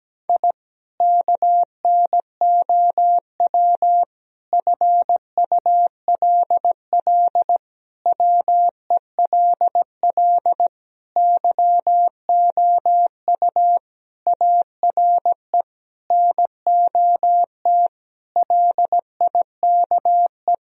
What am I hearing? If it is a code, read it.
Morse code, heard as a keyed tone, about 17 wpm, 700 Hz. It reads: I KNOW FULL WELL YOU ARE NOT LIKE
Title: Sentences from Top 1000 Words - Review Part 3 - 17wpm